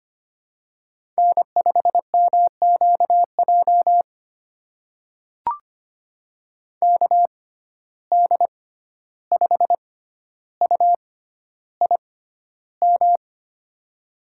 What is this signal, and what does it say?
Morse code, heard as a keyed tone, about 25 wpm, spaced as for 12 wpm, 700 Hz. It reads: N5MQJ KD5UIM
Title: USA Call Signs - 6x Character Spacing - 25wpm